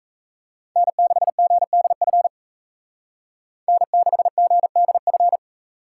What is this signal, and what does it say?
Morse code, heard as a keyed tone, about 35 wpm, 700 Hz. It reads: N6GDF N6GDF